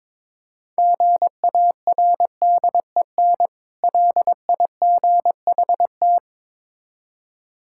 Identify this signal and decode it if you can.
Morse code, heard as a keyed tone, about 22 wpm, 700 Hz. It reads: GARDEN LIGHT
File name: Easy Words - Speed-Racing with Hints - 22wpm